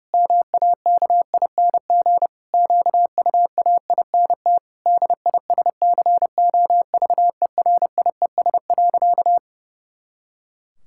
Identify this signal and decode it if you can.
Morse code, heard as a keyed tone, about 32 wpm, 700 Hz. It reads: MAKING QUAINT DISCOVERIES.
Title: Pangram Challenge v2 - 30wpm